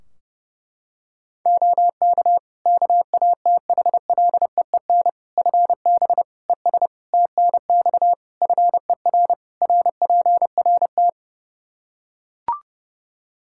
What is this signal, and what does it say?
Morse code, heard as a keyed tone, about 30 wpm, 700 Hz. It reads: OK KATHLEEN FB ES TNX FER RPRT
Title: Practice QSO - Exchange 2 - Info - 30wpm